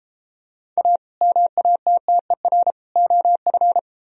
Morse code, heard as a keyed tone, about 32 wpm, 700 Hz. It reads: A MATTER OF